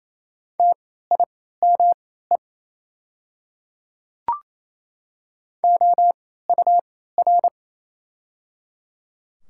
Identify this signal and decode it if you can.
Morse code, heard as a keyed tone, about 27 wpm, spaced as for 16 wpm, 700 Hz. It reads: TIME OUR
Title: 3x Character Spacing - Top 200 Words - 28wpm